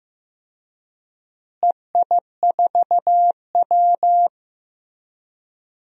Morse code, heard as a keyed tone, about 15 wpm, 700 Hz. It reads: EI4W
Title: DXCC-100 Call Signs - 4 Characters - Repeated 3x - 15wpm